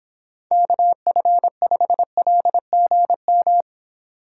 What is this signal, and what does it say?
Morse code, heard as a keyed tone, about 29 wpm, 700 Hz. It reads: KF5LGM